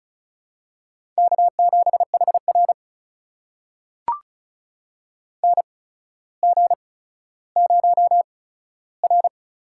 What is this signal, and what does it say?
Morse code, heard as a keyed tone, about 36 wpm, spaced as for 15 wpm, 700 Hz. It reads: K7HR NG0R
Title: USA Call Signs - 8x Character Spacing - 35wpm